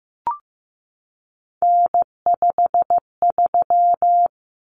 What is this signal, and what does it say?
Morse code, heard as a keyed tone, about 15 wpm, 700 Hz. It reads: N53
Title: Common Cut Numbers - 3-Digit 15wpm